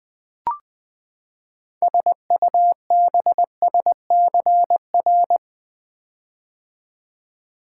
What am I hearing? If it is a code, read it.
Morse code, heard as a keyed tone, about 20 wpm, 700 Hz. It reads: SUBSCR
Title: Build A Word - 6 Letters At A Time - 12-Letter Words - 20wpm